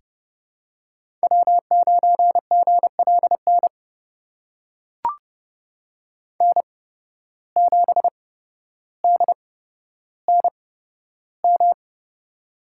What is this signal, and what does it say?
Morse code, heard as a keyed tone, about 30 wpm, spaced as for 13 wpm, 700 Hz. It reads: W9GLN N7DNM
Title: USA Call Signs - 5 Characters - 8x Character Spacing - 30wpm